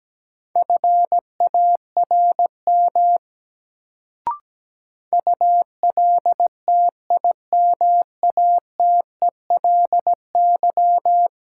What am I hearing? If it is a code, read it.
Morse code, heard as a keyed tone, about 17 wpm, 700 Hz. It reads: FARM ULTIMATELY
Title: Sentences from Top 1200 Words - New Word Drill Alt - 17wpm